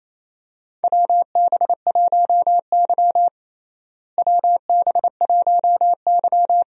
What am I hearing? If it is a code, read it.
Morse code, heard as a keyed tone, about 28 wpm, 700 Hz. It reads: WB1Y WB1Y
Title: USA Call Signs - 4 Characters - Repeated 2x - 28wpm